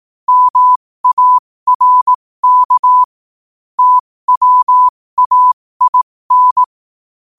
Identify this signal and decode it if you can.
Morse code, heard as a keyed tone, about 18 wpm, spaced as for 15 wpm, 1000 Hz. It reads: MARK TWAIN
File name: qotdfiles / 15wpm